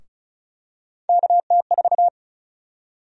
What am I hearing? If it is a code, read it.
Morse code, heard as a keyed tone, about 34 wpm, 700 Hz. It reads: KT4
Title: USA Call Sign Prefixes - Repeated 2x - 35wpm